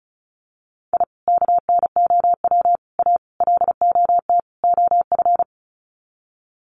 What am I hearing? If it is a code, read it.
Morse code, heard as a keyed tone, about 34 wpm, 700 Hz. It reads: I KNOW A LOT OF
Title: Sets of 5 Words Encore 35wpm